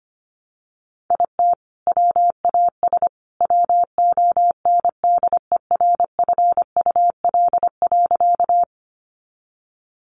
Morse code, heard as a keyed tone, about 25 wpm, 700 Hz. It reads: IT WAS WONDERFUL.
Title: Sentences from Top 1100 Words 25wpm